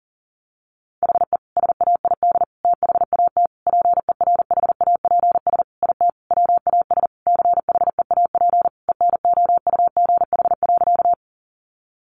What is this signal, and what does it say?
Morse code, heard as a keyed tone, about 40 wpm, 700 Hz. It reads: HE SAID THAT PERHAPS IT WAS CHEAP ENOUGH.
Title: Sentences from Top 1200 Words 40wpm